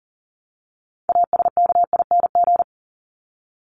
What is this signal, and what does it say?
Morse code, heard as a keyed tone, about 40 wpm, 700 Hz. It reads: ASKING
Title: ING Suffix - Top 100 Words with Base Words 40wpm